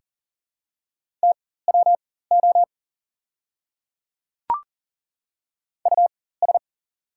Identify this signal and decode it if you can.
Morse code, heard as a keyed tone, about 40 wpm, spaced as for 19 wpm, 700 Hz. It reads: TWO US
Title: 4x Character Spacing - Top 200 Words - 40wpm